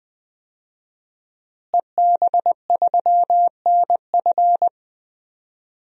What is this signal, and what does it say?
Morse code, heard as a keyed tone, about 21 wpm, 700 Hz. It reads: EB3NF
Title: DXCC-100 Call Signs - 5 Characters - Repeated 3x - 20wpm